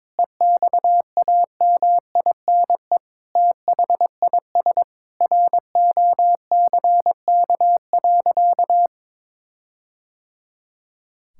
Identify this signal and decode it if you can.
Morse code, heard as a keyed tone, about 22 wpm, 700 Hz. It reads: EXAMINE THIS ROCK.
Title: Sentences from Top 1600 Words 22wpm